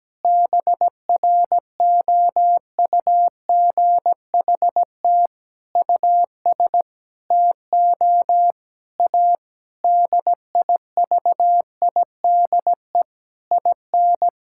Morse code, heard as a keyed tone, about 17 wpm, 700 Hz. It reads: BROUGHT US TO A DIVIDE IN
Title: Sentences from Top 1600 Words 17wpm